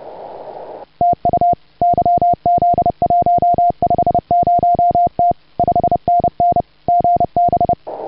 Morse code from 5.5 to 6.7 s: 5NN